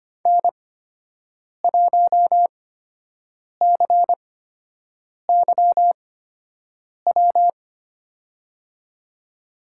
Morse code, heard as a keyed tone, about 25 wpm, spaced as for 7 wpm, 700 Hz. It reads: N1CYW
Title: USA Call Signs - 5 Characters - 8x Character Spacing - 25wpm